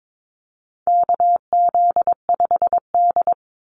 Morse code, heard as a keyed tone, about 22 wpm, 700 Hz. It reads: KZ5D
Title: Call Signs Repeated 2x - 22wpm